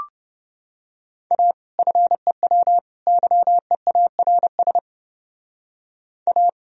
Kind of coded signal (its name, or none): Morse code